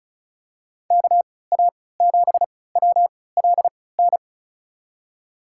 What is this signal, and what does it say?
Morse code, heard as a keyed tone, about 34 wpm, spaced as for 20 wpm, 690 Hz. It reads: KA7WLN
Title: USA Call Signs - 3x Character Spacing - 35wpm